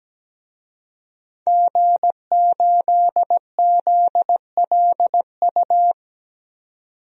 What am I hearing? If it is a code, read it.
Morse code, heard as a keyed tone, about 17 wpm, 700 Hz. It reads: G8ZLU
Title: DXCC-100 Call Signs - 5 Characters - 17wpm